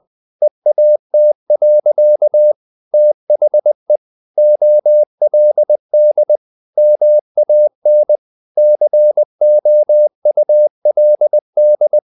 Morse code, heard as a keyed tone, about 20 wpm, 600 Hz. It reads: EAT. THE OLD MAN COULD
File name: the old man and the sea part 2 20wpm